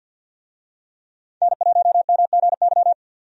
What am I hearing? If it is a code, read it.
Morse code, heard as a keyed tone, about 49 wpm, 700 Hz. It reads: N1MGY
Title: USA Call Signs - 5 Characters - 50wpm